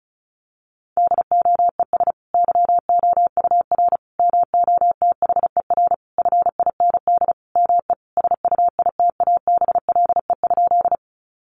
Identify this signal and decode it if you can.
Morse code, heard as a keyed tone, about 34 wpm, 700 Hz. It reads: DOES YOUR MOTHER FIND ME SUITABLE?
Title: Sentences from Top 1400 Words 35wpm